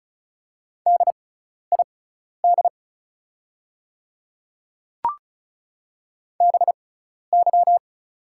Morse code, heard as a keyed tone, about 35 wpm, spaced as for 12 wpm, 700 Hz. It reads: DID BY
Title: 6x Character Spacing - Top 200 Words - 35wpm